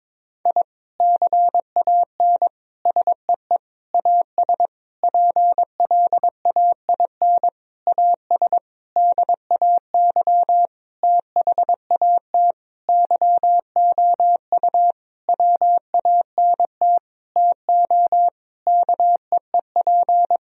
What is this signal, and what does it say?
Morse code, heard as a keyed tone, about 22 wpm, 700 Hz. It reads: I CAN SEE AS PLAIN AS DAY THAT YOU WANT TO KEEP
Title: Sentences from Top 1000 Words - Review Part 2 - 22wpm